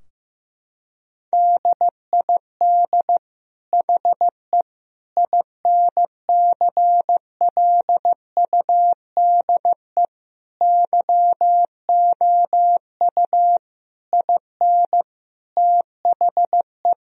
Morse code, heard as a keyed tone, about 15 wpm, 700 Hz. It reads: DID HE INCLUDE YOU IN THE
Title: Sentences from Top 1000 Words - Review Part 4 - 15wpm